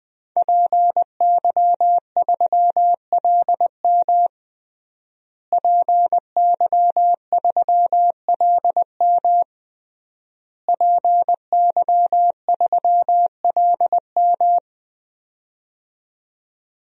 Morse code, heard as a keyed tone, about 20 wpm, 700 Hz. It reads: PY3LM PY3LM PY3LM
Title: DXCC-100 Call Signs - 5 Characters - Repeated 3x - 20wpm